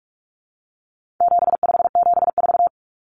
Morse code, heard as a keyed tone, about 44 wpm, 700 Hz. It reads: 7574
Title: Common Cut Numbers - 4-Digit 45wpm